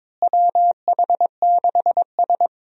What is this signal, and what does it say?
Morse code, heard as a keyed tone, about 22 wpm, 700 Hz. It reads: WH6S
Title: USA Call Signs - 4 Characters - Repeated 2x - 22wpm